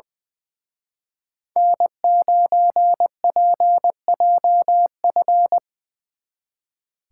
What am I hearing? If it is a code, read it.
Morse code, heard as a keyed tone, about 25 wpm, 700 Hz. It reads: N9PJF
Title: Speed Racing - USA 1x3 Call Signs 20wpm